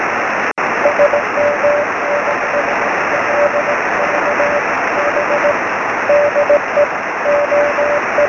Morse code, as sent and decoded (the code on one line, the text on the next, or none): none